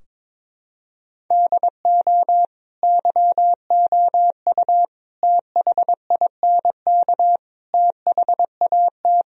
Morse code, heard as a keyed tone, about 22 wpm, 700 Hz. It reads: DO YOU THINK THAT
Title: Sentences from Top 1000 Words - Review Part 4 - 22wpm